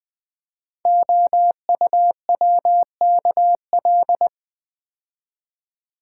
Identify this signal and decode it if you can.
Morse code, heard as a keyed tone, about 20 wpm, 700 Hz. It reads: OUWKL
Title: Code Groups - 5 Letters - 20wpm